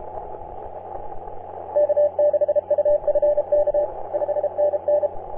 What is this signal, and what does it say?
Morse code, heard as a keyed tone, about 32 wpm, 590 Hz. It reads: K6UFK 5NN
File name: CN8KD on 10m CW